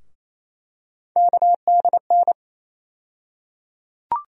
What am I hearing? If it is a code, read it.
Morse code, heard as a keyed tone, about 28 wpm, 700 Hz. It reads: KDN
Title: USA Call Sign Postfixes - 3 Letters 28wpm